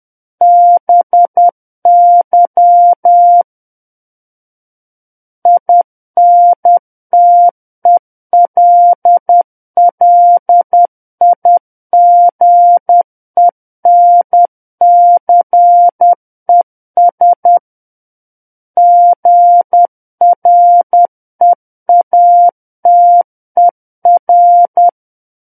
Morse code, heard as a keyed tone, about 10 wpm, 700 Hz. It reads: BY INTELLIGENCES GREATER
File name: Book One / 01-Chapter 1 - 10 WPM